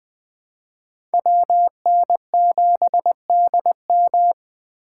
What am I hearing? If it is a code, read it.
Morse code, heard as a keyed tone, about 20 wpm, 700 Hz. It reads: WN7DM
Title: Speed Racing - USA 2x2 Call Signs 17wpm